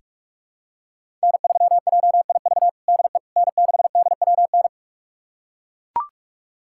Morse code, heard as a keyed tone, about 41 wpm, 700 Hz. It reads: N3JIV DE N6DWN
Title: USA Callsign DE Callsign - 5 Characters - 45wpm